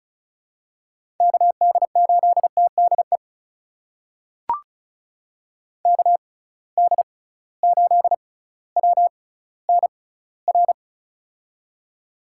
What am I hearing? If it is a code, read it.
Morse code, heard as a keyed tone, about 36 wpm, spaced as for 18 wpm, 700 Hz. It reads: KD8TDE KD8WNR
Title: USA Call Signs - 6 Characters - 6x Character Spacing - 35wpm